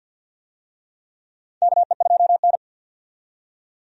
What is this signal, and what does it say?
Morse code, heard as a keyed tone, about 52 wpm, 700 Hz. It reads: KE2N
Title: USA Call Signs - 4 Characters - 50wpm